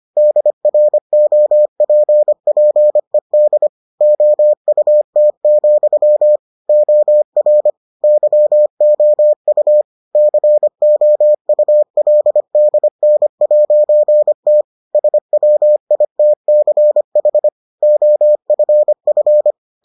Morse code, heard as a keyed tone, about 25 wpm, 600 Hz. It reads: DROPPED OUT, OR YOU COULDN'T SWITCH OFF